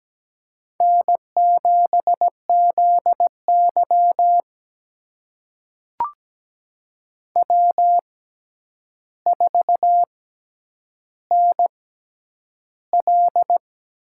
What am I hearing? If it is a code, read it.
Morse code, heard as a keyed tone, about 17 wpm, spaced as for 9 wpm, 700 Hz. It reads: N7ZY W4NL
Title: USA Call Signs - 4 Characters - 6x Character Spacing - 17wpm